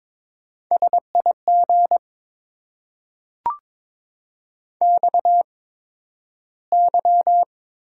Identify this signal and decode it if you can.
Morse code, heard as a keyed tone, about 22 wpm, spaced as for 12 wpm, 700 Hz. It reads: SIG XY
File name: Top 90 QSO Elements - 8x Character Spacing - 22wpm